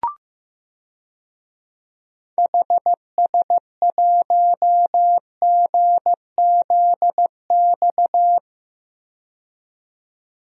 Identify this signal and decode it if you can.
Morse code, heard as a keyed tone, about 15 wpm, 700 Hz. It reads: HS1GZX
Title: DXCC-100 Call Signs - 6 Characters - 15wpm